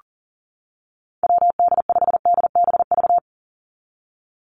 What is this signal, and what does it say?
Morse code, heard as a keyed tone, about 40 wpm, 700 Hz. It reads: WD5DBV
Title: Call Signs - 40wpm